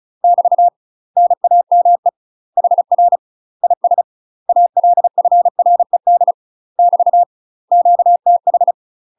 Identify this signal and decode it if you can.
Morse code, heard as a keyed tone, about 34 wpm, 700 Hz. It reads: <BT> NAME HR IS ALFRED <BT> QTH
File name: UK MASTER QSOs TRAINING / 35WPM / UK MASTER QSOs 0030